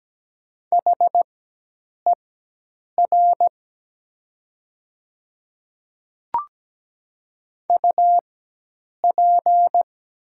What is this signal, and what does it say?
Morse code, heard as a keyed tone, about 17 wpm, spaced as for 8 wpm, 700 Hz. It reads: HER UP
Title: 4x Character Spacing - Top 100 Words - 17wpm